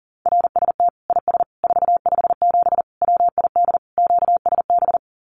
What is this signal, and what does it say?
Morse code, heard as a keyed tone, about 40 wpm, 700 Hz. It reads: RST IS 457 WID QSB
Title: Common QSO Phrases 40wpm